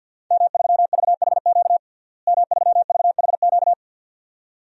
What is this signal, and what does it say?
Morse code, heard as a keyed tone, about 51 wpm, 700 Hz. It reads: M3VHQ M3VHQ
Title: DXCC-100 Call Signs - 5 Characters - Repeated 2x - 50wpm